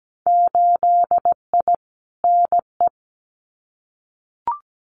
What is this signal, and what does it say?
Morse code, heard as a keyed tone, about 17 wpm, 700 Hz. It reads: 8I NE
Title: Winter Field Day Exchanges Encore 17wpm